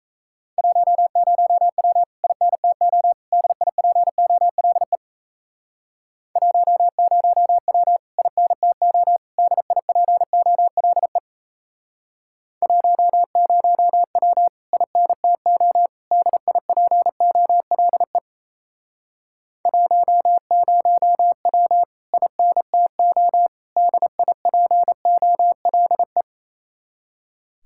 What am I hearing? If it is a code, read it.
Morse code, heard as a keyed tone, about 36 wpm, 700 Hz. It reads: 10W INTO DIPOLE 10W INTO DIPOLE 10W INTO DIPOLE 10W INTO DIPOLE